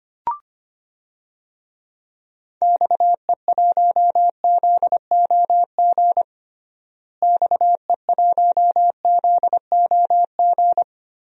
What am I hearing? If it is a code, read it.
Morse code, heard as a keyed tone, about 25 wpm, 700 Hz. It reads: XE1ZOG XE1ZOG
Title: DXCC-100 Call Signs - 6 Characters - Repeated 3x - 25wpm